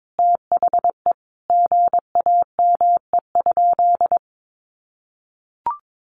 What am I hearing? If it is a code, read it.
Morse code, heard as a keyed tone, about 22 wpm, 700 Hz. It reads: THE GAME?
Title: Sentences from Top 1300 Words 22wpm